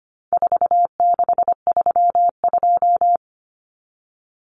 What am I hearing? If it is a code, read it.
Morse code, heard as a keyed tone, about 25 wpm, 700 Hz. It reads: 4632
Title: Common Cut Numbers - 4-Digit 25wpm